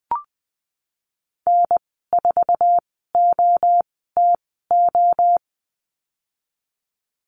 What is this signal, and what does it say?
Morse code, heard as a keyed tone, about 20 wpm, spaced as for 15 wpm, 700 Hz. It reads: N4OTO
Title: 167 Lesson 27g - Number 7 - Call Signs 20wpm